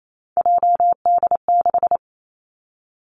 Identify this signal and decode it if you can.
Morse code, heard as a keyed tone, about 28 wpm, 700 Hz. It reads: JD6
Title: Three Challenging Letter-Number Combinations 28wpm